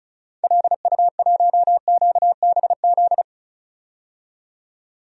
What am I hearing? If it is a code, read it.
Morse code, heard as a keyed tone, about 34 wpm, 700 Hz. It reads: LU1QBZ
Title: DXCC-100 Call Signs - 6 Characters - 35wpm